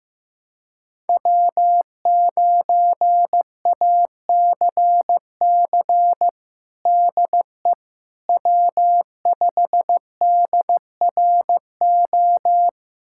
Morse code, heard as a keyed tone, about 16 wpm, 700 Hz. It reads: W9ACC DE W5DRO
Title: USA Callsign DE Callsign - 5 Characters - 15wpm